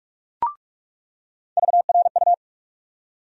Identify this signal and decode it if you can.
Morse code, heard as a keyed tone, about 48 wpm, 700 Hz. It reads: VRU
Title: USA Call Sign Postfixes - 3 Letters 45wpm